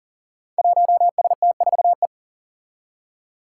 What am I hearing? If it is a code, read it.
Morse code, heard as a keyed tone, about 39 wpm, 700 Hz. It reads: 1ST4E
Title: Five Challenging Letter-Number Combinations 40wpm